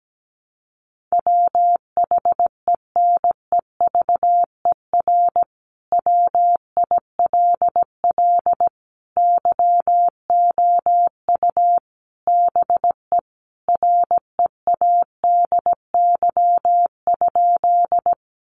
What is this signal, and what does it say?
Morse code, heard as a keyed tone, about 17 wpm, 700 Hz. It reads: WHENEVER WILL YOU BE READY?
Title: Sentences from Top 1200 Words 17wpm